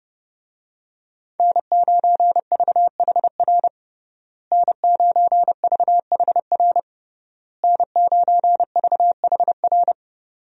Morse code, heard as a keyed tone, about 30 wpm, 700 Hz. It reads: N9VHR N9VHR N9VHR
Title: USA Call Signs - 5 Characters - Repeated 3x - 30wpm